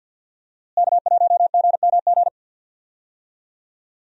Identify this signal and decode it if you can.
Morse code, heard as a keyed tone, about 47 wpm, 700 Hz. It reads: K1GMG
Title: USA Call Signs - 5 Characters - Repeated 2x - 50wpm